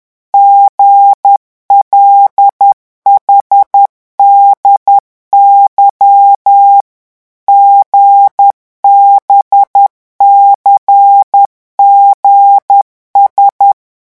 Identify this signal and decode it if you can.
Morse code, heard as a keyed tone, about 10 wpm, 800 Hz. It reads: GLHDY GBCGS
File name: morse mp3 / revision5-vitesse 10 mots